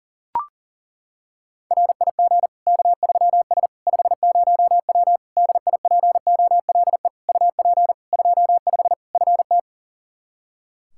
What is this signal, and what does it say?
Morse code, heard as a keyed tone, about 40 wpm, 700 Hz. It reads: RIG K3S 50W DIPOLE UP 25 FT
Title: Practice QSO - Exchange 2 - Info - 40wpm